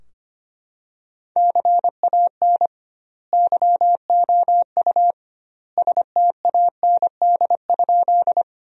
Morse code, heard as a keyed tone, about 25 wpm, 700 Hz. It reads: CAN YOU STAND?
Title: 2x Word-Spacing - Sentences from Top 300 Words - 25wpm